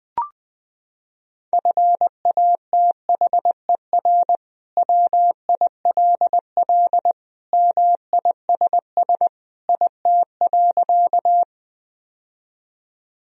Morse code, heard as a keyed tone, about 20 wpm, 700 Hz. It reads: FATHER WILL MISS IT.
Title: Sentences from Top 1000 Words - Review Part 3 - 20wpm